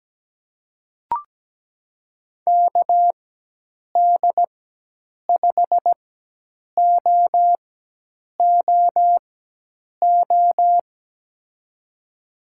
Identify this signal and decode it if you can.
Morse code, heard as a keyed tone, about 17 wpm, spaced as for 8 wpm, 700 Hz. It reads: KD5OOO
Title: USA Call Signs - 6 Characters - 4x Character Spacing - 17wpm